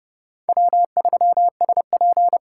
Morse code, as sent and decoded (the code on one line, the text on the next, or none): .-- ...-- ... .--.
W3SP